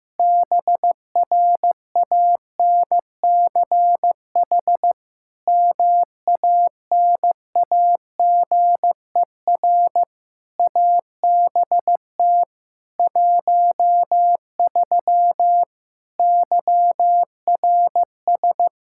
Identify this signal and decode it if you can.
Morse code, heard as a keyed tone, about 15 wpm, 700 Hz. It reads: BRANCH MANAGER ABT 13 YRS
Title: K1USN SST QSO Practice 15wpm